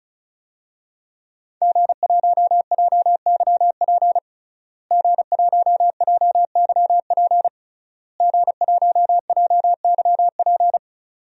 Morse code, heard as a keyed tone, about 34 wpm, 700 Hz. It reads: G1JYP G1JYP G1JYP
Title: DXCC-100 Call Signs - 5 Characters - Repeated 3x - 35wpm